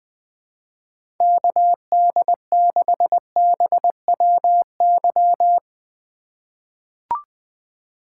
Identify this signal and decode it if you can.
Morse code, heard as a keyed tone, about 20 wpm, 700 Hz. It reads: KD6BWY
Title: USA Call Signs - 6 Characters - 4x Character Spacing - 20wpm